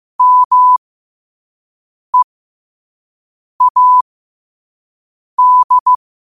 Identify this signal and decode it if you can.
Morse code, heard as a keyed tone, about 15 wpm, spaced as for 5 wpm, 1000 Hz. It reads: MEAD
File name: qotdfiles / 5wpm